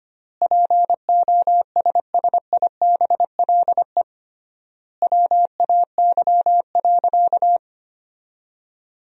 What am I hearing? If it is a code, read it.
Morse code, heard as a keyed tone, about 25 wpm, 700 Hz. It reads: POSSIBLE WAY.
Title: 3x Word-Spacing - Sentences from Top 500 Words - 25wpm